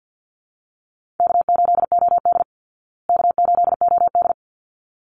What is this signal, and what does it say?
Morse code, heard as a keyed tone, about 48 wpm, 700 Hz. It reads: K8OD K8OD
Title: Easy Call Signs Repeated 2x - 50wpm